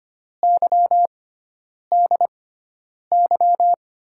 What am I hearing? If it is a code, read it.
Morse code, heard as a keyed tone, about 25 wpm, spaced as for 9 wpm, 700 Hz. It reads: YDY
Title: USA Call Signs - 6 Characters - 6x Character Spacing - 25wpm